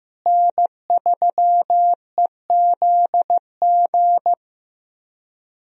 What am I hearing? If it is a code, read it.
Morse code, heard as a keyed tone, about 16 wpm, 700 Hz. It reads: N3EZG
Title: USA Call Signs - 5 Characters - Repeated 3x - 15wpm